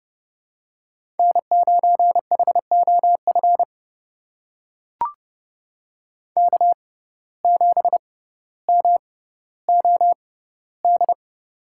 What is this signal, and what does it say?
Morse code, heard as a keyed tone, about 30 wpm, spaced as for 15 wpm, 700 Hz. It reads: N9HOF K7MOD
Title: USA Call Signs - 6x Character Spacing - 30wpm